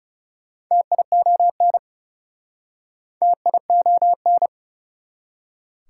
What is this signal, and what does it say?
Morse code, heard as a keyed tone, about 39 wpm, 700 Hz. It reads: TION TION